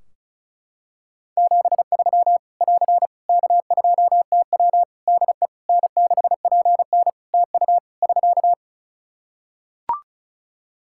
Morse code, heard as a keyed tone, about 34 wpm, 700 Hz. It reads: 73 <AR> K2TW DE N6PN TU <SK>
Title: Practice QSO - Exchange 4 - Ending - 35wpm